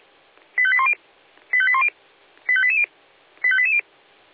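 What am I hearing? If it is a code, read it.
Five-tone selective call (ZVEI), heard as quick runs of five tones, tones of about 70 ms. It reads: 85810 85810 85880 85880
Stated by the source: ZVEI five-tone selective call received by SDR.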